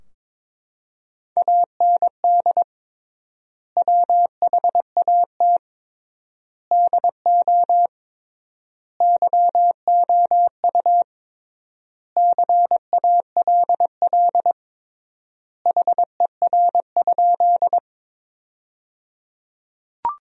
Morse code, heard as a keyed tone, about 21 wpm, 700 Hz. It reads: AND WHAT DO YOU CALL HER?
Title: 3x Word-Spacing - Sentences from Top 100 Words - 22wpm